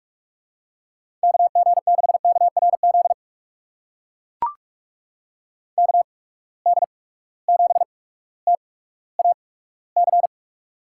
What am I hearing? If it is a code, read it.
Morse code, heard as a keyed tone, about 44 wpm, spaced as for 19 wpm, 700 Hz. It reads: KG6KRZ KD7TAC
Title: USA Call Signs - 6 Characters - 8x Character Spacing - 45wpm